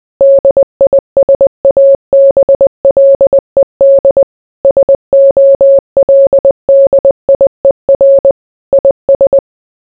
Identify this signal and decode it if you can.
Morse code, heard as a keyed tone, about 20 wpm, 550 Hz. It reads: DISABLED SOLDIER IS